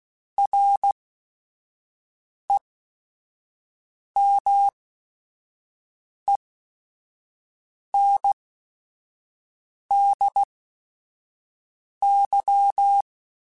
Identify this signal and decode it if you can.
Morse code, heard as a keyed tone, about 16 wpm, spaced as for 5 wpm, 800 Hz. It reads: REMENDY
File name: Quotes / Quotes-mp3s / Temper @5